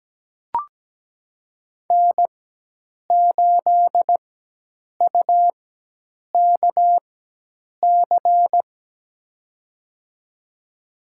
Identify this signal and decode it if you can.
Morse code, heard as a keyed tone, about 17 wpm, spaced as for 8 wpm, 700 Hz. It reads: N8UKC
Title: USA Call Signs - 5 Characters - 4x Character Spacing - 17wpm